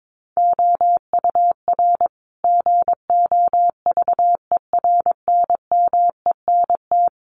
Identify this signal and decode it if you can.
Morse code, heard as a keyed tone, about 22 wpm, 700 Hz. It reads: OUR GOVERNMENT
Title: Sentences from Top 1200 Words 22wpm